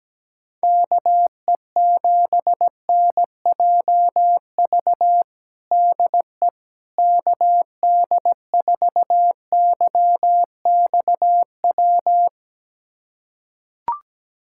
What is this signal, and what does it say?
Morse code, heard as a keyed tone, about 17 wpm, 700 Hz. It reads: KE7NJV DE KD4YXW
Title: USA Callsign DE Callsign 17wpm